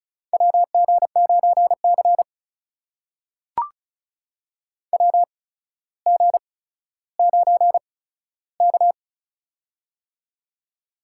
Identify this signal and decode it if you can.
Morse code, heard as a keyed tone, about 36 wpm, spaced as for 15 wpm, 700 Hz. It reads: WG9C WG9K
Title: USA Call Signs - 4 Characters - 8x Character Spacing - 35wpm